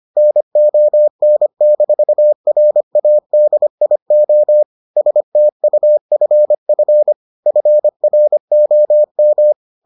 Morse code, heard as a keyed tone, about 25 wpm, 600 Hz. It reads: NON-RADIO STUFF FROM